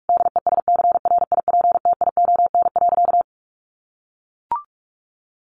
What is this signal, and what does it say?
Morse code, heard as a keyed tone, about 44 wpm, 700 Hz. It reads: DESCRIPTION.
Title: Sentences from Top 1500 Words 45wpm